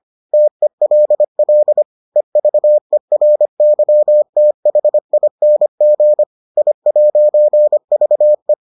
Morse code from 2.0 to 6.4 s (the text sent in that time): EVERYTHING